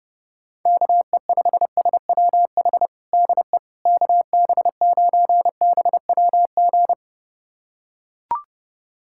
Morse code, heard as a keyed tone, about 31 wpm, 700 Hz. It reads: KE5SWH DE KB9BWG
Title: USA Callsign DE Callsign - 6 Characters - 30wpm